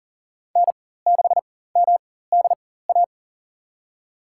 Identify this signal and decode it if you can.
Morse code, heard as a keyed tone, about 38 wpm, spaced as for 18 wpm, 700 Hz. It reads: N6MDA
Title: USA Call Signs - 5 Characters - 4x Character Spacing - 40wpm